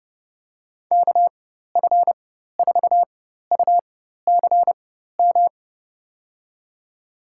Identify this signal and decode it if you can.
Morse code, heard as a keyed tone, about 31 wpm, spaced as for 14 wpm, 700 Hz. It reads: KF4UCM
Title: USA Call Signs - 4x Character Spacing - 30wpm